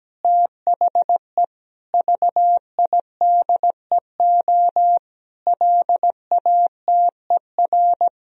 Morse code, heard as a keyed tone, about 17 wpm, 700 Hz. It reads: THE VIDEO LATER
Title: Sentences from Top 1000 Words - Review Part 4 - 17wpm